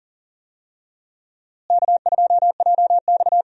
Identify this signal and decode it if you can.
Morse code, heard as a keyed tone, about 40 wpm, 700 Hz. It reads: K2JX
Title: USA Call Signs - 4 Characters - Repeated 2x - 40wpm